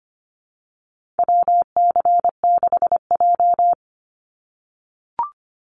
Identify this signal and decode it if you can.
Morse code, heard as a keyed tone, about 25 wpm, 700 Hz. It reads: WC6J
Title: NA SOTA Activator Callsigns 25wpm